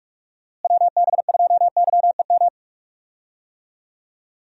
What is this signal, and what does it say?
Morse code, heard as a keyed tone, about 43 wpm, 700 Hz. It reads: WB2YEM
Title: USA Call Signs - 6 Characters - Repeated 3x - 45wpm